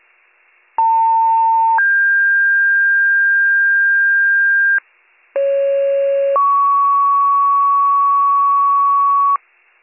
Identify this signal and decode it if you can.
Two-tone page, heard as a one-second tone, then a three-second tone, about 900 then 1600 Hz, about 550 then 1120 Hz, with noise. It is heard from 0.8 to 9.4 s.